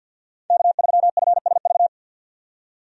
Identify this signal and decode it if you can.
Morse code, heard as a keyed tone, about 52 wpm, 700 Hz. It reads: K3FSV
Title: USA Call Signs - 5 Characters - 50wpm